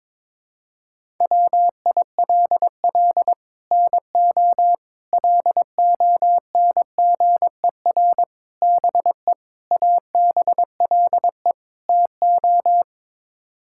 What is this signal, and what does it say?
Morse code, heard as a keyed tone, about 21 wpm, 700 Hz. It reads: WILL NO LONGER BE ABLE TO
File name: practice / 22wpm / Sets of 6 Words - 22wpm